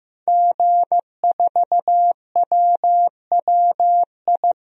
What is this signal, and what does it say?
Morse code, heard as a keyed tone, about 15 wpm, 700 Hz. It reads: G4WWI
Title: DXCC-100 Call Signs - 5 Characters - Repeated 3x - 15wpm